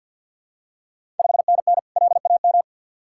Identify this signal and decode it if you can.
Morse code, heard as a keyed tone, about 48 wpm, 690 Hz. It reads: 5NN LAM
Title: Ontario QSO Party Exchange 50wpm